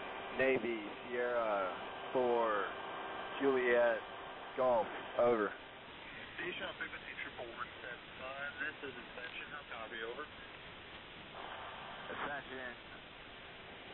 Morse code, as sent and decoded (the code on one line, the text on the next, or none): none